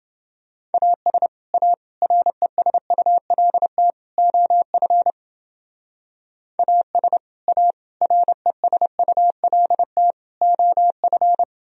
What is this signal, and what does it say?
Morse code, heard as a keyed tone, about 28 wpm, 700 Hz. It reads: AS A RESULT OF AS A RESULT OF